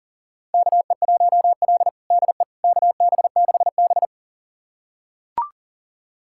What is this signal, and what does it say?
Morse code, heard as a keyed tone, about 40 wpm, 700 Hz. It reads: KE1L DE KB6B
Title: USA Callsign DE Callsign - 4 Characters - 40wpm